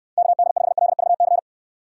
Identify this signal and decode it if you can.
Morse code, heard as a keyed tone, about 69 wpm, 700 Hz. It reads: KD5LVC